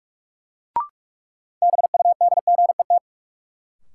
Morse code, heard as a keyed tone, about 46 wpm, 700 Hz. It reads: BUDGET
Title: Sentences from Top 1200 Words - New Word Drill - 45wpm